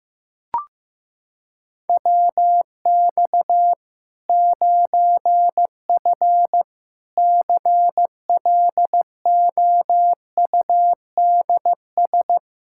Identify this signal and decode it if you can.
Morse code, heard as a keyed tone, about 15 wpm, 700 Hz. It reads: WX 9F CLOUDS
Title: Practice QSO - Exchange 2 - Info - 15wpm